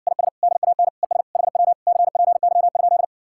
Morse code, heard as a keyed tone, about 70 wpm, 700 Hz. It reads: IS DAN ES HW CPY?